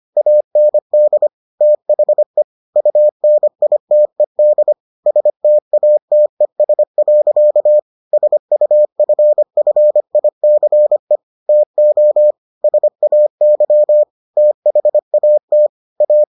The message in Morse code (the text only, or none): AND THE UNITED STATES. SUFFICE TO SAY THAT A